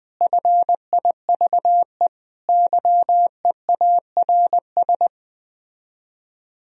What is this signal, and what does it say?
Morse code, heard as a keyed tone, about 21 wpm, 700 Hz. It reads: FIVE YEARS